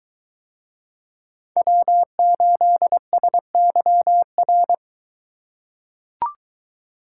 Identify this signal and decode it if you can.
Morse code, heard as a keyed tone, about 23 wpm, 700 Hz. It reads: W8SYR